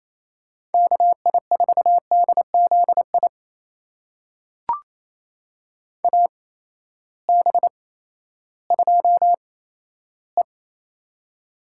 Morse code, heard as a keyed tone, about 28 wpm, spaced as for 14 wpm, 700 Hz. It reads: KI4DZI AB2E
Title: USA Call Signs - 8x Character Spacing - 28wpm